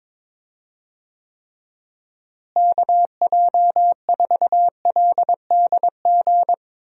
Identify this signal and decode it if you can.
Morse code, heard as a keyed tone, about 21 wpm, 700 Hz. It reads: KJ4LDG
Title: USA Call Signs - 6 Characters - Repeated 3x - 22wpm